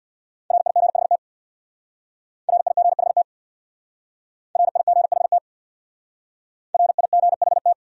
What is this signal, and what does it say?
Morse code, heard as a keyed tone, about 68 wpm, 700 Hz. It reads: RIGHT RIGHT RIGHT RIGHT